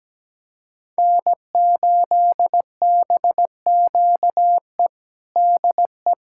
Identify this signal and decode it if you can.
Morse code, heard as a keyed tone, about 17 wpm, 700 Hz. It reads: N8BQE DE
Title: USA Callsign DE Callsign 17wpm